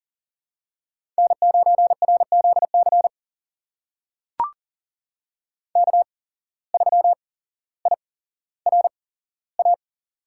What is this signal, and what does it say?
Morse code, heard as a keyed tone, about 40 wpm, spaced as for 17 wpm, 700 Hz. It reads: N9RZC K3IRA
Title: USA Call Signs - 5 Characters - 8x Character Spacing - 40wpm